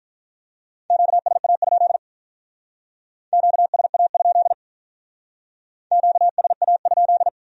Morse code, heard as a keyed tone, about 44 wpm, 700 Hz. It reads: QSA? QSA? QSA?
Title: Speed-Racing Q-Codes - 35wpm